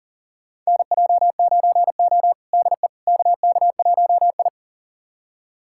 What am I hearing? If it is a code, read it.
Morse code, heard as a keyed tone, about 40 wpm, 700 Hz. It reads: NJ9O DE KK1I
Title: USA Callsign DE Callsign - 4 Characters - 40wpm